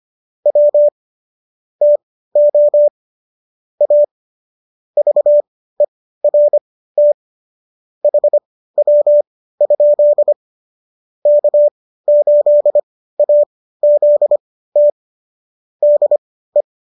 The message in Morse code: W TO A VERT HW? K8AZT DE

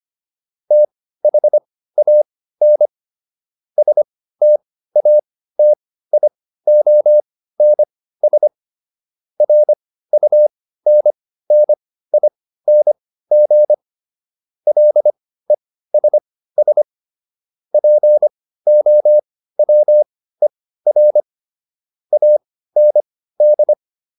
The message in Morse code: THAN STATIONS RUNNING LESS POWER AND

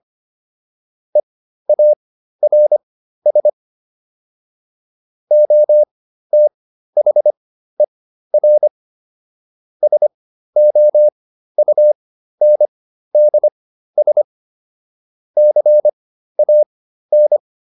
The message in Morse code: EARS OTHER SOUNDS CAN

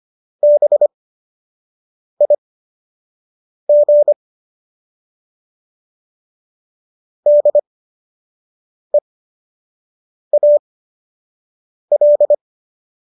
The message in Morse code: BIG DEAL

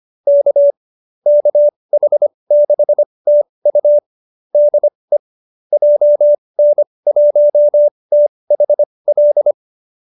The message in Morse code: K KH6TU DE JN1THL

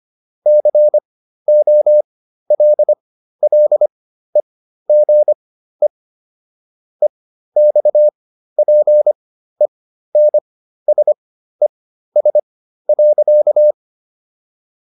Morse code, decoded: COLLEGE EXPENSES.

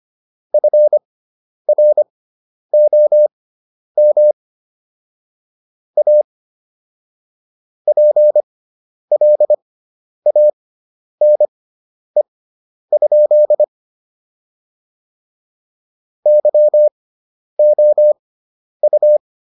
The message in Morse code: FROM A PLANE? YOU

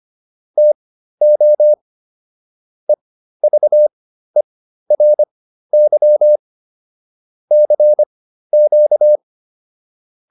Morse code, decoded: TO EVERY CQ

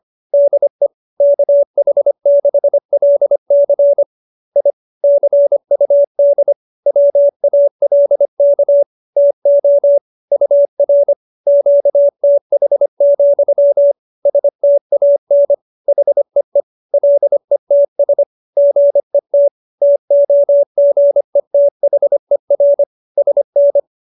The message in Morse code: DE KH6LC I CUD WALK TO UR QTH, STAN HEE LETS GET TOGETHER SN

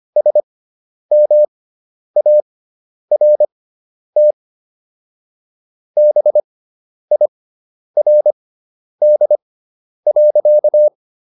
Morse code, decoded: SMART BIRD.

